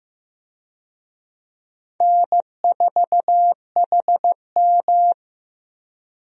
Morse code, heard as a keyed tone, about 16 wpm, 700 Hz. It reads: N4HM
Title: USA Call Signs - 4 Characters - Repeated 3x - 15wpm